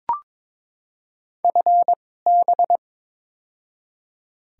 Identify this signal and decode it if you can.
Morse code, heard as a keyed tone, about 22 wpm, spaced as for 16 wpm, 690 Hz. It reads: FB
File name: practice / 22wpm / Top 20 QSO Elements - 2x Character Spacing - 22wpm